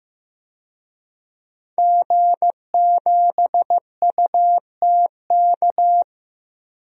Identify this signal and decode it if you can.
Morse code, heard as a keyed tone, about 15 wpm, 700 Hz. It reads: G7UTK
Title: DXCC-100 Call Signs - 5 Characters - Repeated 2x - 15wpm